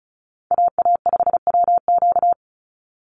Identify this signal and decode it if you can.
Morse code, heard as a keyed tone, about 34 wpm, 700 Hz. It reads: AA5WQ
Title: USA Call Signs 35wpm